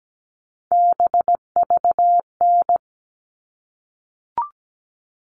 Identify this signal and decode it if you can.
Morse code, heard as a keyed tone, about 17 wpm, 700 Hz. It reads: BVN
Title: All Cut Numbers - 3-Digit 17wpm